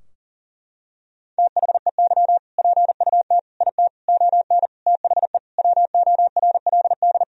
Morse code, heard as a keyed tone, about 40 wpm, 700 Hz. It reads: THEY PUT IT ON THE WORLD
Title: Sentences from Top 1000 Words - Review Part 4 - 40wpm